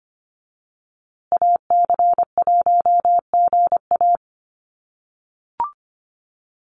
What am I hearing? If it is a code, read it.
Morse code, heard as a keyed tone, about 25 wpm, 700 Hz. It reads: AC1GA
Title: USA Call Signs Repeated 2x 25wpm